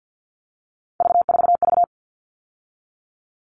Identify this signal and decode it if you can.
Morse code, heard as a keyed tone, about 48 wpm, 700 Hz. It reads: V4V
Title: All Cut Numbers - 3-Digit 50wpm